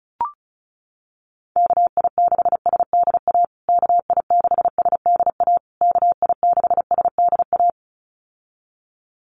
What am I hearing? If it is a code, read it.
Morse code, heard as a keyed tone, about 34 wpm, 700 Hz. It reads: KI6SDA KI6SDA KI6SDA
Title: USA Call Signs Repeated 3x 35wpm